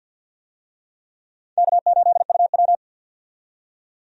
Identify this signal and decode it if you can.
Morse code, heard as a keyed tone, about 50 wpm, 700 Hz. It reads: K8UW
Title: USA Call Signs - 4 Characters - 50wpm